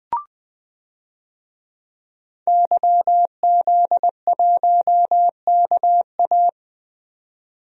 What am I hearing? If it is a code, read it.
Morse code, heard as a keyed tone, about 20 wpm, 700 Hz. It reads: YZ1KA
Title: DXCC-100 Call Signs - 5 Characters - 20wpm